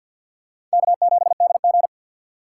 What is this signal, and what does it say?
Morse code, heard as a keyed tone, about 52 wpm, 690 Hz. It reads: K7DG
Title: USA Call Signs - 3x Character Spacing - 50wpm